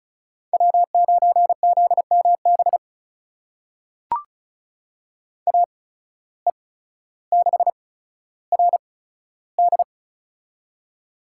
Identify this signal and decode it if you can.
Morse code, heard as a keyed tone, about 36 wpm, spaced as for 15 wpm, 700 Hz. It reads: W9ZMB AE6RD
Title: USA Call Signs - 5 Characters - 8x Character Spacing - 35wpm